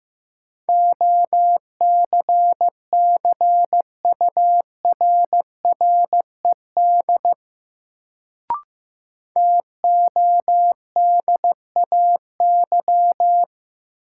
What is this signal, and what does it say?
Morse code, heard as a keyed tone, about 16 wpm, 700 Hz. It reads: OCCURRED TODAY
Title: Sets of 5 Words Encore - New Word Drill - 15wpm